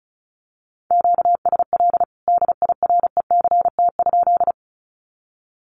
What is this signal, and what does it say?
Morse code, heard as a keyed tone, about 34 wpm, 700 Hz. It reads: QSL DIRECT?
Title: Common QSO Phrases 35wpm